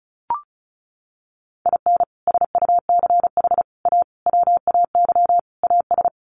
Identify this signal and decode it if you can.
Morse code, heard as a keyed tone, about 34 wpm, 700 Hz. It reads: IN SUCH A WAY AS
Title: Sets of 5 Words Encore 35wpm